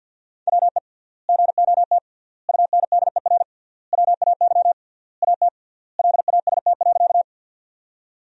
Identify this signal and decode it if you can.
Morse code, heard as a keyed tone, about 50 wpm, 700 Hz. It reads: WE GOT UNDER WAY AT LAST.
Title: 3x Word-Spacing - Sentences from Top 300 Words - 50wpm